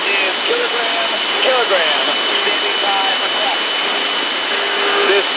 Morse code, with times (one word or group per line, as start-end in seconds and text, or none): none